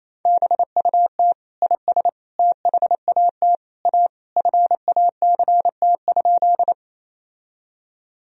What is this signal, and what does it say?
Morse code, heard as a keyed tone, about 28 wpm, 700 Hz. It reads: BUT IS THAT A FACT?
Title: Sentences from Top 1000 Words - Review Part 3 - 28wpm